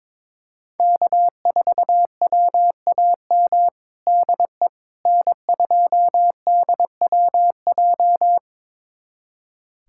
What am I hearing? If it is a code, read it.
Morse code, heard as a keyed tone, about 22 wpm, 700 Hz. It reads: K4WAM DE N2DWJ
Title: USA Callsign DE Callsign - 5 Characters - 22wpm